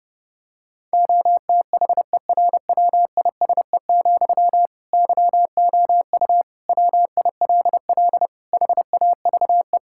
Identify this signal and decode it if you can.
Morse code, heard as a keyed tone, about 32 wpm, 700 Hz. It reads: OTHERWISE, YOU WILL HAVE
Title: Sentences from Top 1600 Words 30wpm